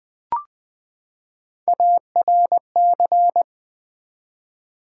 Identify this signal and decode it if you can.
Morse code, heard as a keyed tone, about 20 wpm, 700 Hz. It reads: ARC
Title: Build A Word - 3 Letters At A Time - 9-Letter Words - 20wpm